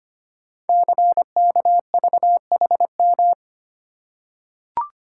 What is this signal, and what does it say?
Morse code, heard as a keyed tone, about 25 wpm, 700 Hz. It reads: CKVHM
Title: Code Groups - 5 Letters - 25wpm